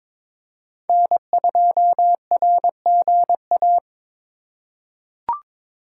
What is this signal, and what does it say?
Morse code, heard as a keyed tone, about 22 wpm, 700 Hz. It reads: N2RGA
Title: USA Call Signs - 4x Character Spacing - 22wpm